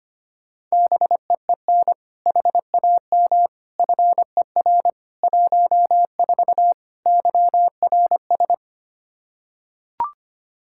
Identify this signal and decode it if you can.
Morse code, heard as a keyed tone, about 25 wpm, 700 Hz. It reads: BEEN HAM FER 14 YRS
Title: Practice QSO - Exchange 3 - More Info - 25wpm